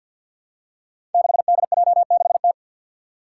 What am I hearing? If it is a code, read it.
Morse code, heard as a keyed tone, about 52 wpm, 700 Hz. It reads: 6DJ6T
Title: Five Challenging Letter-Number Combinations 50wpm